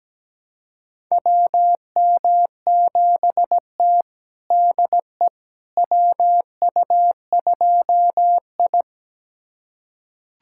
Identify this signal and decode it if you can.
Morse code, heard as a keyed tone, about 17 wpm, 700 Hz. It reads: WM7T DE WU2I
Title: USA Callsign DE Callsign - 4 Characters - 17wpm